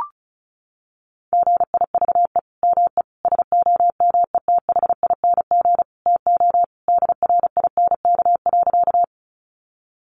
Morse code, heard as a keyed tone, about 34 wpm, 700 Hz. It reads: GIVE ME SOMETHING TO DRINK.
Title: Sentences from Top 1500 Words 35wpm